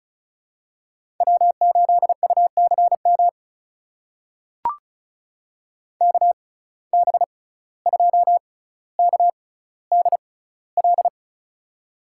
Morse code, heard as a keyed tone, about 36 wpm, spaced as for 17 wpm, 700 Hz. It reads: W8UCM KB2KDL
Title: USA Call Signs - 6x Character Spacing - 35wpm